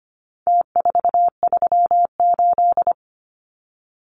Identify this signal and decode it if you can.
Morse code, heard as a keyed tone, about 25 wpm, 700 Hz. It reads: T438
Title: Numbers - Contest Style - 4-Digit 25wpm